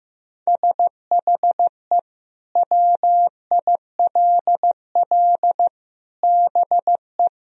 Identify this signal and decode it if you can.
Morse code, heard as a keyed tone, about 16 wpm, 700 Hz. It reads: SHE WILL BE
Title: Sentences from Top 1000 Words - Review Part 2 - 15wpm